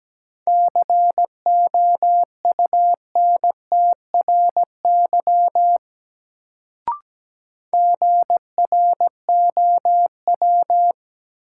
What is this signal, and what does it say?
Morse code, heard as a keyed tone, about 17 wpm, 700 Hz. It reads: COUNTRY GROW
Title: Sentences from Top 300 Words - New Word Drill - 17wpm